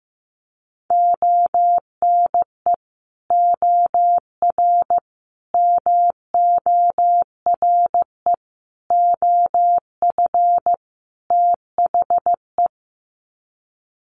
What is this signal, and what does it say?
Morse code, heard as a keyed tone, about 15 wpm, 700 Hz. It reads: ONE OR MORE OF THE
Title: Sets of 5 Words Encore 15wpm